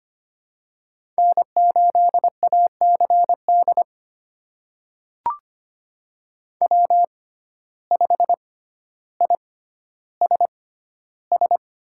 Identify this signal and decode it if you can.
Morse code, heard as a keyed tone, about 25 wpm, spaced as for 13 wpm, 700 Hz. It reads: N8ACD W5ISS
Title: USA Call Signs - 5 Characters - 6x Character Spacing - 25wpm